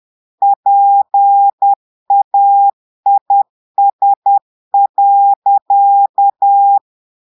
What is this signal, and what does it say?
Morse code, heard as a keyed tone, about 10 wpm, 800 Hz. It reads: PAIS.
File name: Portugais - texte en clair 1 - vitesse - 10 mots minute